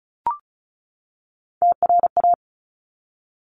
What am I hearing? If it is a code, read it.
Morse code, heard as a keyed tone, about 34 wpm, 700 Hz. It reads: TRA
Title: Top Three Letter Combinations in Words 35wpm